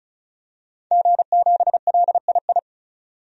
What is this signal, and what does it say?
Morse code, heard as a keyed tone, about 34 wpm, 700 Hz. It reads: G7LII
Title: DXCC-100 Call Signs - 5 Characters - 35wpm